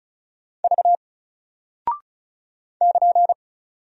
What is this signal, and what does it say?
Morse code, heard as a keyed tone, about 34 wpm, 700 Hz. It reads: V <KN>
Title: Single Character (Letter:Number:Punctuation:Prosign) Alt 35wpm